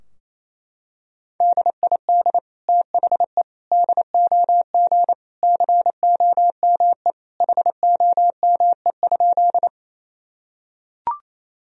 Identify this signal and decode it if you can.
Morse code, heard as a keyed tone, about 28 wpm, 700 Hz. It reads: DID THE DOG COME HOME?
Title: Sentences from Top 1000 Words - Review Part 2 - 28wpm